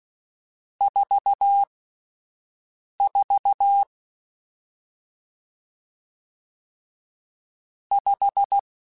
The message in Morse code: 44 5

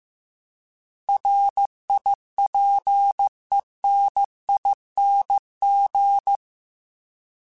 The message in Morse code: RIPENING

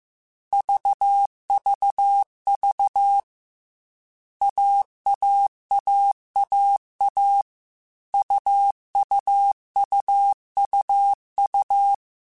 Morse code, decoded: VVV AAAAA UUUUU

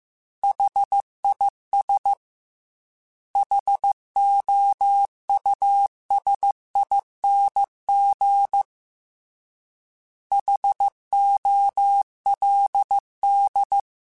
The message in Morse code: HIS HOUSING HOLD

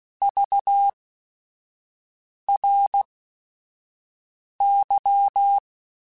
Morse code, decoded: VRY

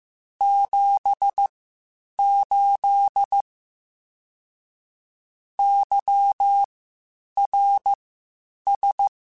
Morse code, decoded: 78 YRS

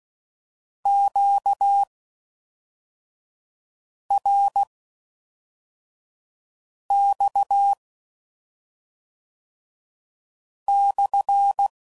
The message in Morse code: QRX/

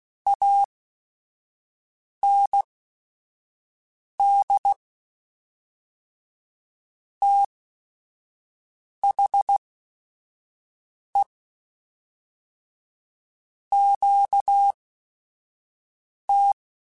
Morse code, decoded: AND THE QT

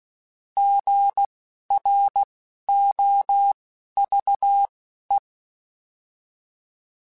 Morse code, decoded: GROVE

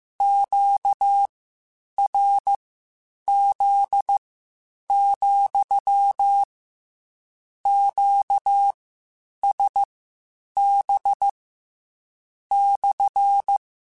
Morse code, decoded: QRZ, QSB /

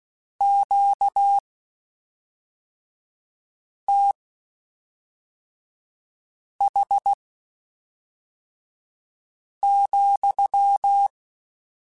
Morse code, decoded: QTH,

